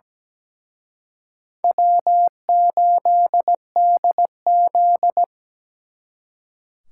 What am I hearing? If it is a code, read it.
Morse code, heard as a keyed tone, about 20 wpm, 700 Hz. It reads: W8DZ
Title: Speed Racing - USA Call Signs 17wpm